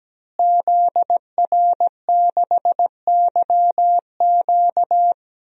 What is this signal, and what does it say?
Morse code, heard as a keyed tone, about 17 wpm, 700 Hz. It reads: ZR6YQ
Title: DXCC-100 Call Signs - 5 Characters - Repeated 3x - 17wpm